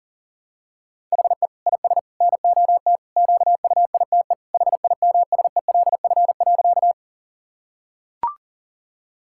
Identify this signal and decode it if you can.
Morse code, heard as a keyed tone, about 40 wpm, 700 Hz. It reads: HE IS NOT QUITE HIMSELF.
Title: Sentences from Top 1600 Words 40wpm